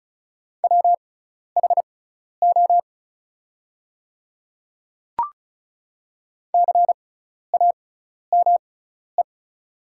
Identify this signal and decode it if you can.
Morse code, heard as a keyed tone, about 36 wpm, spaced as for 12 wpm, 690 Hz. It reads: WHO CAME
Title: 6x Character Spacing - Top 300 Words - 35wpm